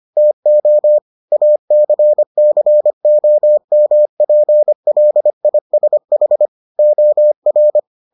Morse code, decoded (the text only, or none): TO ACCOMPLISH OR